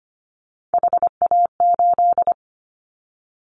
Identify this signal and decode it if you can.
Morse code, heard as a keyed tone, about 25 wpm, 700 Hz. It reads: HA8
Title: Callsign Prefixes for Top 25 DXCC - 25wpm